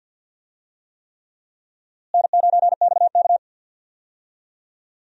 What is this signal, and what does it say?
Morse code, heard as a keyed tone, about 50 wpm, 700 Hz. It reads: N9XK
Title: USA Call Signs - 4 Characters - Repeated 3x - 50wpm